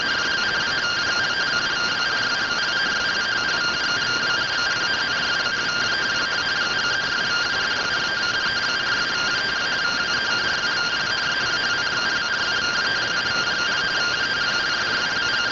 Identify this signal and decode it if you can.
Two-tone teleprinter signal, about 170 Hz shift.